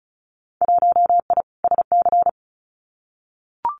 Morse code, heard as a keyed tone, about 36 wpm, 700 Hz. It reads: 1I SC
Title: Winter Field Day Exchanges Encore 35wpm